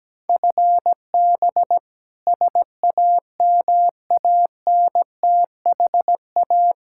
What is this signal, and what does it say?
Morse code, heard as a keyed tone, about 17 wpm, 700 Hz. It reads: FB SAMANTHA
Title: K1USN SST QSO Practice 17wpm